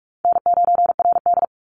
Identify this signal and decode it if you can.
Morse code, heard as a keyed tone, about 44 wpm, 700 Hz. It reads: N9RD
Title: Easy Call Signs - 45wpm